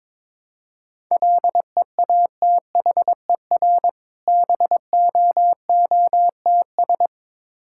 Morse code, heard as a keyed tone, about 22 wpm, 690 Hz. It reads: LEATHER BOOTS
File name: Difficult Words with Hints - 22wpm